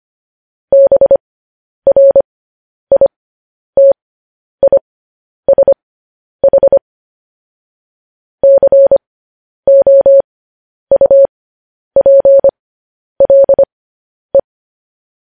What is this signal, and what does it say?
Morse code, heard as a keyed tone, about 25 wpm, spaced as for 10 wpm, 550 Hz. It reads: BRITISH COUPLE
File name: podcast / News headlines-1x-25x10